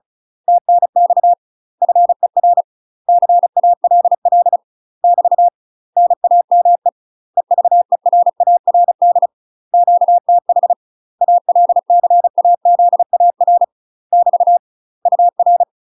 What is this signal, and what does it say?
Morse code, heard as a keyed tone, about 35 wpm, 700 Hz. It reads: TNX FER CALL <BT> NAME EVERARD QTH ALCAZAR <BT> UR
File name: UK MASTER QSOs TRAINING / 35WPM / UK MASTER QSOs 0022